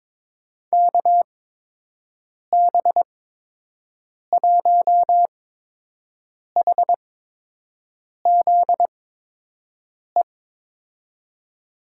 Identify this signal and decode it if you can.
Morse code, heard as a keyed tone, about 22 wpm, spaced as for 6 wpm, 700 Hz. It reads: KB1HZE
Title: USA Call Signs - 6 Characters - 8x Character Spacing - 22wpm